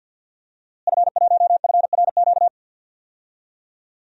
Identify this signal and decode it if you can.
Morse code, heard as a keyed tone, about 52 wpm, 700 Hz. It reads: F1FRQ
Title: DXCC-100 Call Signs - 5 Characters - Repeated 3x - 50wpm